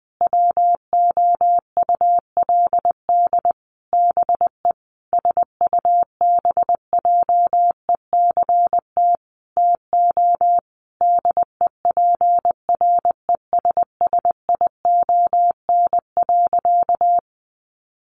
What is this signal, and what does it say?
Morse code, heard as a keyed tone, about 20 wpm, 700 Hz. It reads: WOULD BE SUBJECT TO DEPRESSION.
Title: Sentences from Top 1300 Words 20wpm